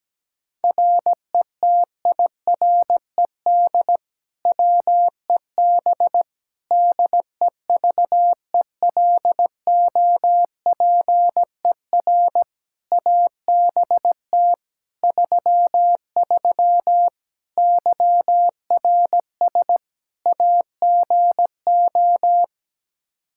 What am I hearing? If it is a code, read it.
Morse code, heard as a keyed tone, about 17 wpm, 700 Hz. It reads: RETIRED WEB DEVELOPER ABT 33 YRS AGO
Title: K1USN SST QSO Practice 17wpm